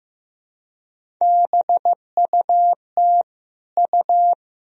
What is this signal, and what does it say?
Morse code, heard as a keyed tone, about 16 wpm, 700 Hz. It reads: BUT U
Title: Sentences from Top 300 Words and Abbreviations 15wpm